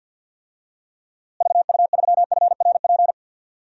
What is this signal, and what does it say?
Morse code, heard as a keyed tone, about 52 wpm, 700 Hz. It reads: VU3FRP
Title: DXCC-100 Call Signs - 6 Characters - Repeated 3x - 50wpm